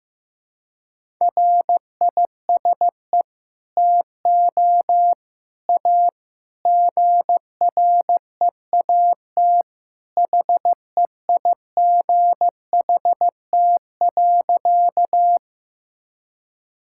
Morse code, heard as a keyed tone, about 15 wpm, 700 Hz. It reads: RISE TO A GREAT HEIGHT.
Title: Sentences from Top 1600 Words 15wpm